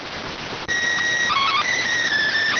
SSTV (slow-scan television), PD 290.